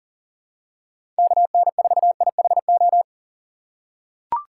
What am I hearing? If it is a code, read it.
Morse code, heard as a keyed tone, about 40 wpm, 700 Hz. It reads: KN4IHO
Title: USA Call Signs - 6 Characters - 6x Character Spacing - 40wpm